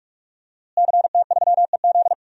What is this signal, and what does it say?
Morse code, heard as a keyed tone, about 42 wpm, 700 Hz. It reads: CT3EZ
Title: DXCC-100 Call Signs - 5 Characters - Repeated 3x - 45wpm